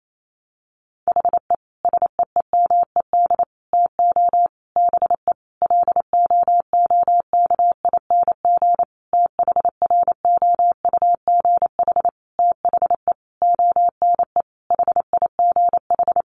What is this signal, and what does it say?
Morse code, heard as a keyed tone, about 28 wpm, 700 Hz. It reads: HE SEEMED TO BE LOOKING THROUGH THE ONE HIGH
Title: A Sound Of Thunder 28wpm